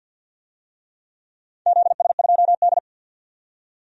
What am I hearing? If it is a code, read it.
Morse code, heard as a keyed tone, about 51 wpm, 700 Hz. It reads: ZS2D
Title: DXCC-100 Call Signs - 4 Characters - 50wpm